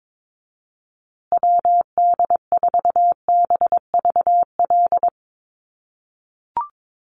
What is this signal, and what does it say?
Morse code, heard as a keyed tone, about 22 wpm, 700 Hz. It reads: WD4BVL
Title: USA Call Signs Repeated 2x 22wpm